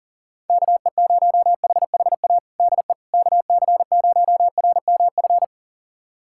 Morse code, heard as a keyed tone, about 40 wpm, 700 Hz. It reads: KE0HHA DE KC0RMF
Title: USA Callsign DE Callsign 40wpm